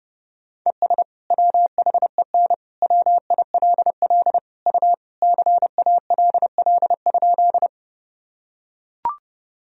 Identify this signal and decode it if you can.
Morse code, heard as a keyed tone, about 30 wpm, 700 Hz. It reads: ES WHEN WILL U CALL?
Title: Sentences from Top 100 Words and Abbreviations 30wpm